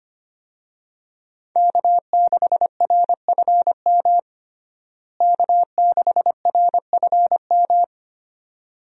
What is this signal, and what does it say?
Morse code, heard as a keyed tone, about 25 wpm, 700 Hz. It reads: K6RFM K6RFM
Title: USA Call Signs - 5 Characters - Repeated 2x - 25wpm